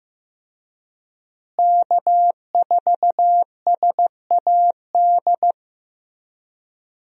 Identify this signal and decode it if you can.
Morse code, heard as a keyed tone, about 15 wpm, 700 Hz. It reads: K4SAD
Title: USA Call Signs - 5 Characters - Repeated 3x - 15wpm